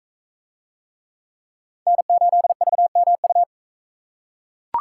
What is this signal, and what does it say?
Morse code, heard as a keyed tone, about 41 wpm, 700 Hz. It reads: N8VMU